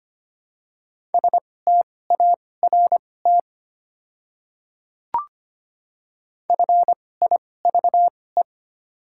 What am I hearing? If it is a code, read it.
Morse code, heard as a keyed tone, about 25 wpm, spaced as for 18 wpm, 700 Hz. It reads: START FIVE
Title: 2x Character Spacing - Top 500 Words - 25wpm